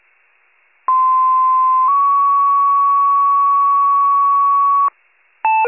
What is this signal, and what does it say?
Two-tone page, heard as a one-second tone, then a three-second tone, about 1040 then 1120 Hz. It is heard from 0.9 to 4.9 s.